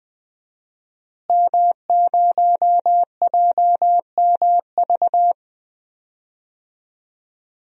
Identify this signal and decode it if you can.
Morse code, heard as a keyed tone, about 21 wpm, 700 Hz. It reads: M0JMV
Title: DXCC-100 Call Signs - 5 Characters - Repeated 3x - 20wpm